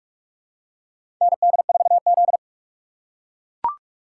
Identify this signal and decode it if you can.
Morse code, heard as a keyed tone, about 43 wpm, 700 Hz. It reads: ND4Z
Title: USA Call Signs - 4 Characters - 3x Character Spacing - 45wpm